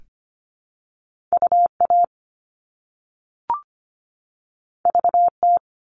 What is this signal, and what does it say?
Morse code, heard as a keyed tone, about 25 wpm, 700 Hz. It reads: UA VT
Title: All Cut Numbers - 2-Digit 25wpm